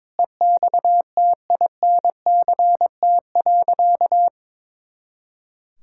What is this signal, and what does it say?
Morse code, heard as a keyed tone, about 22 wpm, 700 Hz. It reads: EXTINCT.
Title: Pangram Challenge v2 - 22wpm